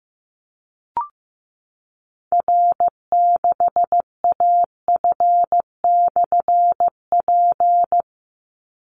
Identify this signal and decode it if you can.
Morse code, heard as a keyed tone, about 15 wpm, 700 Hz. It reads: R6AF/P
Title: Call Signs Repeated 2x - 15wpm